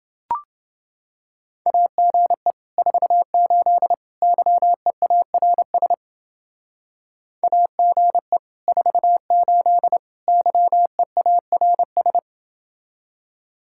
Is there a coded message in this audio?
Yes: Morse code from 1.7 to 12.2 s, heard as a keyed tone, about 29 wpm, 700 Hz. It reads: AGE 48 YEARS AGE 48 YEARS